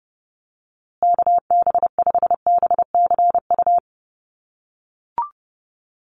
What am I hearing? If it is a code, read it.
Morse code, heard as a keyed tone, about 30 wpm, 700 Hz. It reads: KB5BCU
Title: USA Call Signs Repeated 2x 30wpm